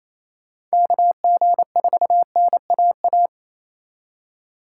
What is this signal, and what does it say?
Morse code, heard as a keyed tone, about 35 wpm, 700 Hz. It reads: KG4NAA
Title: Speed Racing - USA 2x3 Call Signs 28wpm